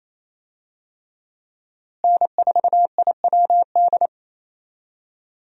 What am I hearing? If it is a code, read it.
Morse code, heard as a keyed tone, about 28 wpm, 700 Hz. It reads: N4IWD
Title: USA Call Signs - 5 Characters - Repeated 3x - 28wpm